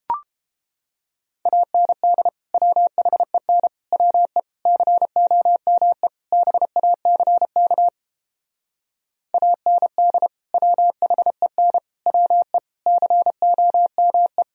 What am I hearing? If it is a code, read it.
Morse code, heard as a keyed tone, about 29 wpm, 700 Hz. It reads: AND WHEN WE COME BACK AND WHEN WE COME